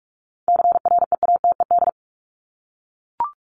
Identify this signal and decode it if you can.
Morse code, heard as a keyed tone, about 44 wpm, 700 Hz. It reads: CREATED
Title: ED Suffix - Top 100 Words with Base Words 45wpm